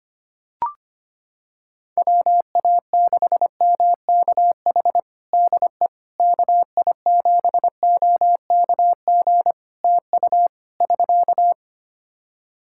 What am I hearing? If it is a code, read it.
Morse code, heard as a keyed tone, about 25 wpm, 700 Hz. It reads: WA6MKH DE KI7OKG TU <SK>
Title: Practice QSO - Exchange 4 - Ending - 25wpm